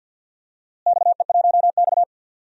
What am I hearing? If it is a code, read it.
Morse code, heard as a keyed tone, about 53 wpm, 700 Hz. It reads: XE1X
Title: DXCC-100 Call Signs - 4 Characters - Repeated 3x - 50wpm